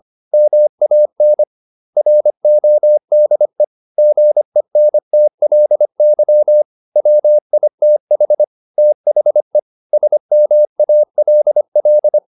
Morse code, heard as a keyed tone, about 25 wpm, 600 Hz. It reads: MAN RODE GENTLY WITH THE SMALL
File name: the old man and the sea part 1 25wpm